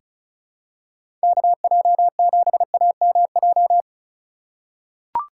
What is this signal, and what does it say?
Morse code, heard as a keyed tone, about 34 wpm, 700 Hz. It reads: KJ7AMJ
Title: USA Call Signs - 6 Characters - 3x Character Spacing - 35wpm